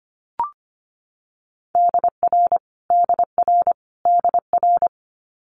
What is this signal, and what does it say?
Morse code, heard as a keyed tone, about 25 wpm, 700 Hz. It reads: DR DR DR
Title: Multi-letter Phonemes Repeated 3x 25wpm